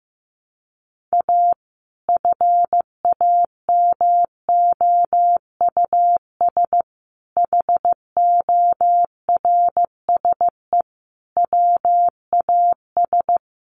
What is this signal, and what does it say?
Morse code, heard as a keyed tone, about 15 wpm, 700 Hz. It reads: A FAMOUS HORSE WAS
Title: Sentences from Top 1500 Words 15wpm